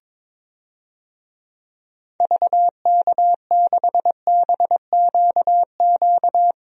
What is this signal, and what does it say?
Morse code, heard as a keyed tone, about 22 wpm, 700 Hz. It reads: VK6BQQ
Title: DXCC-100 Call Signs - 6 Characters - Repeated 2x - 22wpm